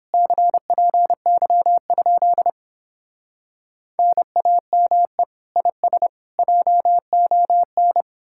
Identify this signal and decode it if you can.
Morse code, heard as a keyed tone, about 29 wpm, 700 Hz. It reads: CPY? NAME IS JON